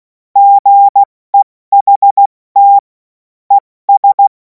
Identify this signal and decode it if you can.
Morse code, heard as a keyed tone, about 16 wpm, spaced as for 14 wpm, 800 Hz. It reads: GEHT ES
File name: Morsekurs / Witze / Witz 28. April 2026 70 F80